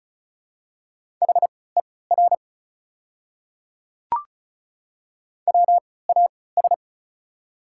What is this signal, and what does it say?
Morse code, heard as a keyed tone, about 34 wpm, spaced as for 20 wpm, 690 Hz. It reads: HER WAS
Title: 3x Character Spacing - Top 100 Words - 35wpm